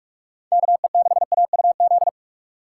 Morse code, heard as a keyed tone, about 44 wpm, 700 Hz. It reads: KE6AUZ